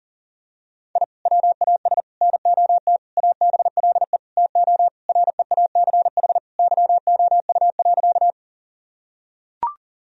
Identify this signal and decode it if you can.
Morse code, heard as a keyed tone, about 40 wpm, 700 Hz. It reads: I WAS NOT ABLE TO REACH YOU.
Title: Sentences from Top 1000 Words - Review Part 2 - 40wpm